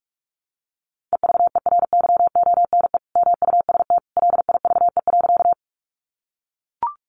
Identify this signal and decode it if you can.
Morse code, heard as a keyed tone, about 44 wpm, 700 Hz. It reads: EVERYONE MUST LIVE.
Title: Sentences from Top 800 Words 45wpm